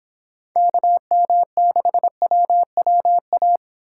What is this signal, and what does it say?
Morse code, heard as a keyed tone, about 26 wpm, 700 Hz. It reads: KM6WWA